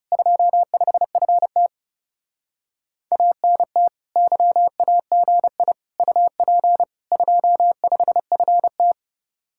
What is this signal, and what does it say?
Morse code, heard as a keyed tone, about 37 wpm, 700 Hz. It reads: 25FT ANT YAGI UP 25FT
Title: Speed-Racing Common QSO Phrases - 30wpm